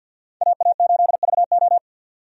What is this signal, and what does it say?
Morse code, heard as a keyed tone, about 52 wpm, 700 Hz. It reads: AA8VO
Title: USA Call Signs - 5 Characters - 50wpm